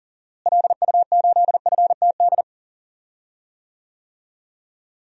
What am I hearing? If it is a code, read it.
Morse code, heard as a keyed tone, about 40 wpm, 700 Hz. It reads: LU8FTD
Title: DXCC-100 Call Signs - 6 Characters - Repeated 2x - 40wpm